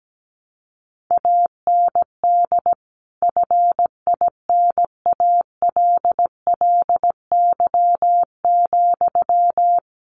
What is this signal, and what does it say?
Morse code, heard as a keyed tone, about 17 wpm, 700 Hz. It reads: AND FINALLY,
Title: Sentences from Top 1100 Words 17wpm